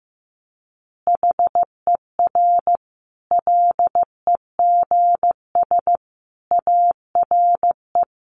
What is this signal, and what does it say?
Morse code, heard as a keyed tone, about 15 wpm, 700 Hz. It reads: HER LEGS ARE
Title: Easy Phrases 15wpm